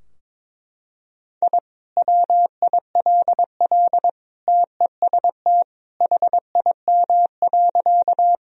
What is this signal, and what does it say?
Morse code, heard as a keyed tone, about 22 wpm, 700 Hz. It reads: I WILL TEST HIM.
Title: Sentences from Top 1000 Words - Review Part 3 - 22wpm